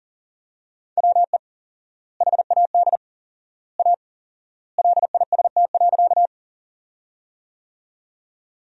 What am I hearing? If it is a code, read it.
Morse code, heard as a keyed tone, about 40 wpm, 700 Hz. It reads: WE HAD A LIST.
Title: 4x Word-Spacing - Sentences from Top 400 Words - 40wpm